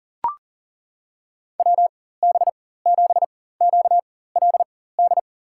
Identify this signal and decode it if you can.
Morse code, heard as a keyed tone, about 40 wpm, spaced as for 19 wpm, 700 Hz. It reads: WB7QLD
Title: USA Call Signs - 4x Character Spacing - 40wpm